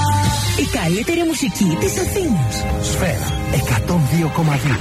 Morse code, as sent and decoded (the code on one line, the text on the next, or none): none